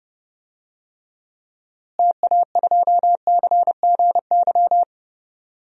Morse code, heard as a keyed tone, about 32 wpm, 700 Hz. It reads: TA2CGY
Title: DXCC-100 Call Signs - 6 Characters - Repeated 2x - 30wpm